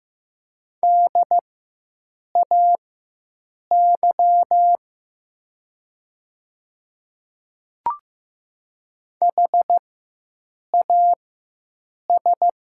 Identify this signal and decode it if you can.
Morse code, heard as a keyed tone, about 15 wpm, spaced as for 7 wpm, 700 Hz. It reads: DAY HAS
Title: 4x Character Spacing - Top 100 Words - 15wpm